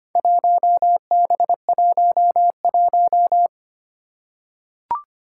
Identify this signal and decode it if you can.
Morse code, heard as a keyed tone, about 25 wpm, 700 Hz. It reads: 1B11
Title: Four Challenging Letter-Number Combinations 25wpm